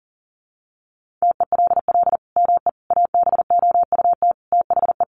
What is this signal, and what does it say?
Morse code, heard as a keyed tone, about 40 wpm, 700 Hz. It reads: TELL ME ABOUT THE
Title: Sentences from Top 1500 Words 40wpm